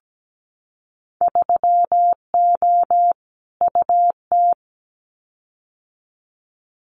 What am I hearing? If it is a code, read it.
Morse code, heard as a keyed tone, about 17 wpm, 700 Hz. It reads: 3O UT
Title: Winter Field Day Exchanges 17wpm